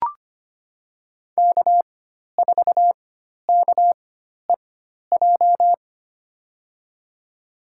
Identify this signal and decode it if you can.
Morse code, heard as a keyed tone, about 25 wpm, spaced as for 12 wpm, 700 Hz. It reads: K4KEJ
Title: USA Call Signs - 5 Characters - 4x Character Spacing - 25wpm